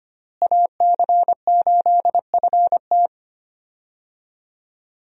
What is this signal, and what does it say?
Morse code, heard as a keyed tone, about 25 wpm, 700 Hz. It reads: AC8FT
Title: USA Call Signs - 5 Characters - Repeated 3x - 25wpm